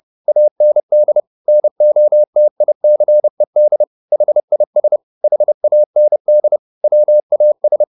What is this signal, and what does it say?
Morse code, heard as a keyed tone, about 30 wpm, 600 Hz. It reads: AND NOTICED HIS HAND WAS
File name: the old man and the sea / the old man and the sea 1 30wpm